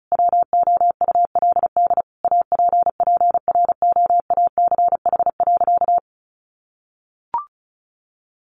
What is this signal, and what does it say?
Morse code, heard as a keyed tone, about 34 wpm, 700 Hz. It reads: WOULD APPROACH.
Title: Sentences from Top 1400 Words 35wpm